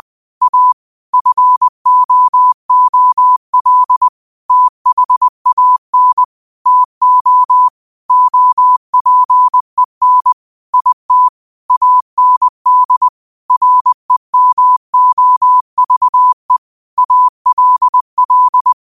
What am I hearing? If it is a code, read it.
Morse code, heard as a keyed tone, about 20 wpm, 1000 Hz. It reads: A FOOL THAN TO OPEN IT AND REMOVE ALL